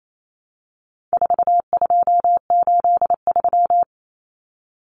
Morse code, heard as a keyed tone, about 28 wpm, 700 Hz. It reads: 4283
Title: Common Cut Numbers - 4-Digit 28wpm